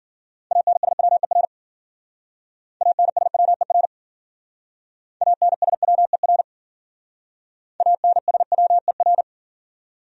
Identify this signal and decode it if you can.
Morse code, heard as a keyed tone, about 52 wpm, 700 Hz. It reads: ANSWER ANSWER ANSWER ANSWER